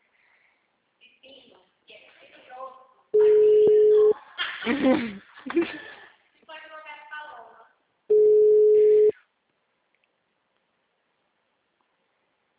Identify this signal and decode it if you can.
Ringback tone (Europe, 425 Hz), with speech.
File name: out- -2005- - - .9120